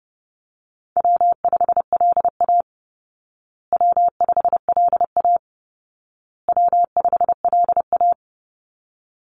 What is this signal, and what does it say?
Morse code, heard as a keyed tone, about 30 wpm, 700 Hz. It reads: W5LA W5LA W5LA
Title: Call Signs Repeated 2x - 30wpm